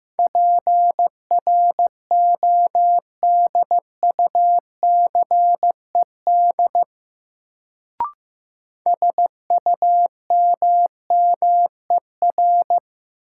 Morse code, heard as a keyed tone, about 15 wpm, 700 Hz. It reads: PRODUCED SUMMER
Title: Sentences from Top 1000 Words - Review Part 6 - New Word Drill - 15wpm